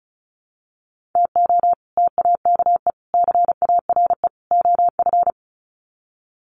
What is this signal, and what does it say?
Morse code, heard as a keyed tone, about 34 wpm, 700 Hz. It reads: TO TAKE CARE OF
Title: Sets of 4 Words 35wpm